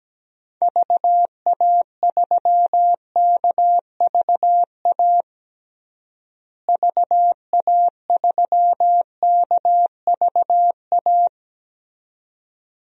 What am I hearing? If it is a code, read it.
Morse code, heard as a keyed tone, about 17 wpm, 700 Hz. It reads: VA3KVA VA3KVA
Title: DXCC-100 Call Signs - 6 Characters - Repeated 2x - 17wpm